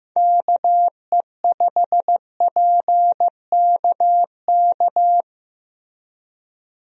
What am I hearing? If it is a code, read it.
Morse code, heard as a keyed tone, about 15 wpm, 700 Hz. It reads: KE5PKK
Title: USA Call Signs - 6 Characters - Repeated 2x - 15wpm